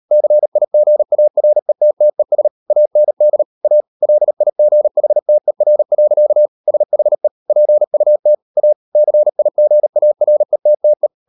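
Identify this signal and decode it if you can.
Morse code, heard as a keyed tone, about 38 wpm, 600 Hz. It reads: CIGARETTES AND A LIGHTER. SHE PUT A CIGARETTE
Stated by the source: Morse code practice at 38 WPM, sending Ken Follett's novel Never in English, from CW med Gnister.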